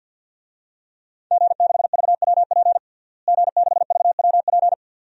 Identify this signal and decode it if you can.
Morse code, heard as a keyed tone, about 51 wpm, 700 Hz. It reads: G6VWP G6VWP
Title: DXCC-100 Call Signs - 5 Characters - Repeated 2x - 50wpm